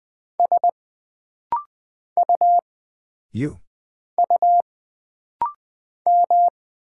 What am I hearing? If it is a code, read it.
Morse code, heard as a keyed tone, about 20 wpm, 700 Hz. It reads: S U U M
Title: Single Character (Letter:Number:Punctuation:Prosign) - Rapid-Fire - 20wpm